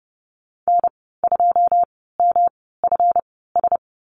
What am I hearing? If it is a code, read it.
Morse code, heard as a keyed tone, about 30 wpm, spaced as for 17 wpm, 700 Hz. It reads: N2MFS
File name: Lesson 37g - Letter x - Call Signs 30wpm